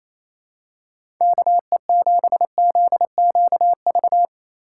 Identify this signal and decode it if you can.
Morse code, heard as a keyed tone, about 28 wpm, 700 Hz. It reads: KE7ZQV
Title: USA Call Signs - 6 Characters - 3x Character Spacing - 28wpm